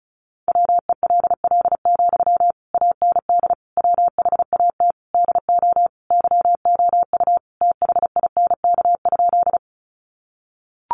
Morse code, heard as a keyed tone, about 34 wpm, 700 Hz. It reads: WELL, AND WHAT DO YOU THINK?
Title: Sentences from Top 200 Words 35wpm